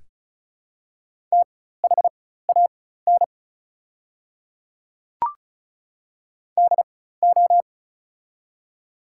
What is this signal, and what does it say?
Morse code, heard as a keyed tone, about 34 wpm, spaced as for 16 wpm, 700 Hz. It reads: THAN DO
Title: 4x Character Spacing - Top 100 Words - 35wpm